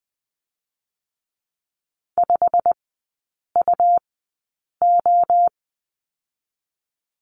Morse code, heard as a keyed tone, about 20 wpm, 700 Hz. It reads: 5 U O
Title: 100 Lesson 17e - Letter u - Three Characters 20wpm